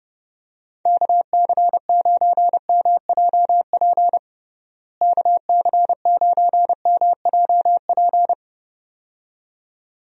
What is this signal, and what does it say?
Morse code, heard as a keyed tone, about 32 wpm, 700 Hz. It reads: KC9MJP KC9MJP
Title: USA Call Signs - 6 Characters - Repeated 2x - 30wpm